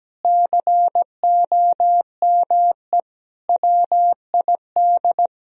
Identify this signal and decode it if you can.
Morse code, heard as a keyed tone, about 17 wpm, 700 Hz. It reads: COME WID
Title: Sentences from Top 200 Words and Abbreviations 17wpm